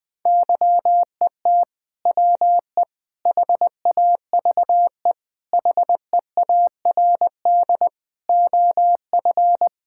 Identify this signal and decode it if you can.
Morse code, heard as a keyed tone, about 20 wpm, 700 Hz. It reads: YET WE HAVE HEARD OF
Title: Sentences from Top 1000 Words - Review Part 5 - 20wpm